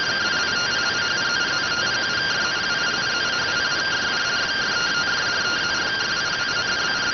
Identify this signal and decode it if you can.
Two-tone teleprinter signal, about 170 Hz shift.